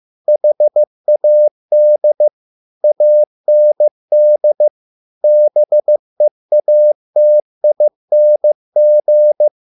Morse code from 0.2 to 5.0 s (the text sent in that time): HAD AND